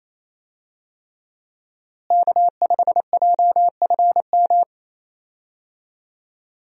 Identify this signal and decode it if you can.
Morse code, heard as a keyed tone, about 28 wpm, 700 Hz. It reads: K5JFM
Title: USA Call Signs - 5 Characters - Repeated 3x - 28wpm